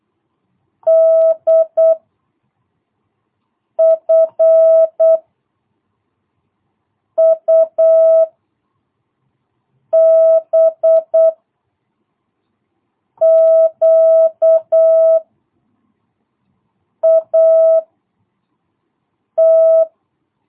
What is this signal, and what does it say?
Morse code, heard as a keyed tone, about 8 wpm, 660 Hz. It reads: D F U B Q A T
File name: jh8cbh / 2022cw